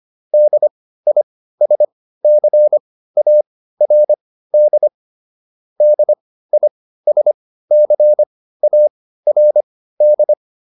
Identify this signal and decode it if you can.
Morse code, heard as a keyed tone, about 25 wpm, spaced as for 15 wpm, 600 Hz. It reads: DISCARD DISCARD